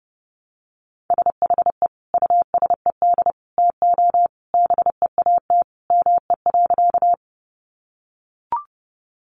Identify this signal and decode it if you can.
Morse code, heard as a keyed tone, about 30 wpm, 700 Hz. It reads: SHE USED TO BEAT ME.
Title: Sentences from Top 1300 Words 30wpm